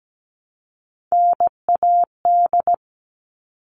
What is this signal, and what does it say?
Morse code, heard as a keyed tone, about 17 wpm, 700 Hz. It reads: NAD
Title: All Cut Numbers - 3-Digit 17wpm